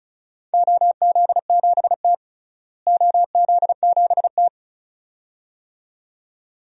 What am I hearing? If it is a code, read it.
Morse code, heard as a keyed tone, about 34 wpm, 700 Hz. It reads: OZ7T OZ7T
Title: DXCC-100 Call Signs - 4 Characters - Repeated 2x - 35wpm